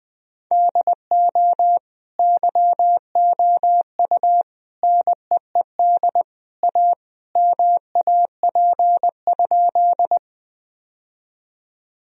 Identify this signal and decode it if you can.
Morse code, heard as a keyed tone, about 20 wpm, 700 Hz. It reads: DO YOU NEED A MAP?
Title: Sentences from Top 1000 Words - Review Part 2 - 20wpm